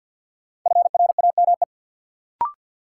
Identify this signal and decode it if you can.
Morse code, heard as a keyed tone, about 53 wpm, 700 Hz. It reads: FRAME
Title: Sentences from Top 1400 Words - New Word Drill Alt - 50wpm